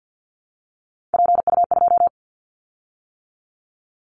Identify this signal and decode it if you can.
Morse code, heard as a keyed tone, about 50 wpm, 700 Hz. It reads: PU2
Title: Callsign Prefixes for Top 100 DXCC - 50wpm